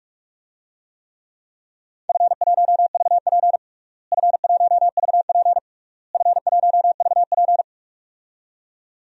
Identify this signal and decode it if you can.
Morse code, heard as a keyed tone, about 46 wpm, 700 Hz. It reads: F1VP F1VP F1VP
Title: DXCC-100 Call Signs - 4 Characters - Repeated 3x - 45wpm